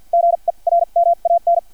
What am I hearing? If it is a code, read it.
Morse code, heard as a keyed tone, about 50 wpm, spaced as for 39 wpm, 680 Hz. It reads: GERMAN
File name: archived / CRSnet